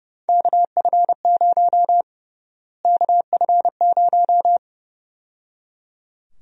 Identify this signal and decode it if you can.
Morse code, heard as a keyed tone, about 30 wpm, 700 Hz. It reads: KF0 KF0
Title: USA Call Sign Prefixes - Repeated 2x - 30wpm